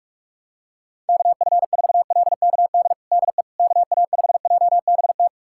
Morse code, heard as a keyed tone, about 41 wpm, 700 Hz. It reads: KF4LKD DE KA5JBT
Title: USA Callsign DE Callsign - 6 Characters - 45wpm